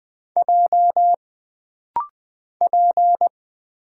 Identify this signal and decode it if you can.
Morse code, heard as a keyed tone, about 20 wpm, 700 Hz. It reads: J P
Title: Single Letter-Number Alt - Rapid-Fire 20wpm